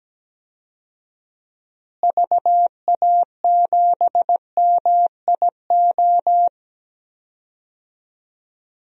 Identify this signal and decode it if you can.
Morse code, heard as a keyed tone, about 17 wpm, 700 Hz. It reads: VA7MIO
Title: DXCC-100 Call Signs - 6 Characters - Repeated 3x - 17wpm